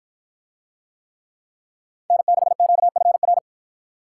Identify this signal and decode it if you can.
Morse code, heard as a keyed tone, about 53 wpm, 700 Hz. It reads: N6QFR